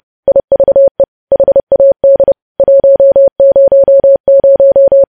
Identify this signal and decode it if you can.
Morse code, heard as a keyed tone, about 30 wpm, 550 Hz. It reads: IVE HAD 100
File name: podcast / News headlines-1x-30x30